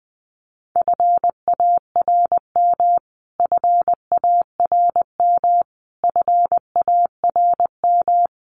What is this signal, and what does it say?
Morse code, heard as a keyed tone, about 20 wpm, 700 Hz. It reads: FARM FARM FARM
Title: Top 500 Words Repeated 20wpm